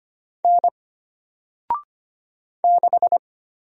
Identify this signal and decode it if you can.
Morse code, heard as a keyed tone, about 25 wpm, 700 Hz. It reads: N 6
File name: Single-Letter-Number-25wpm